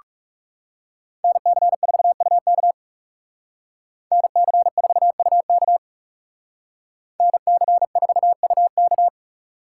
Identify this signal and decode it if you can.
Morse code, heard as a keyed tone, about 35 wpm, 700 Hz. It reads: NC4UK NC4UK NC4UK
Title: Speed Racing - USA 2x2 Call Signs 30wpm